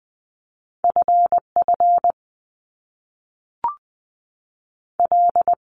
Morse code, heard as a keyed tone, about 20 wpm, 700 Hz. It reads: F F L
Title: Top Two Letter Combinations in Words 20wpm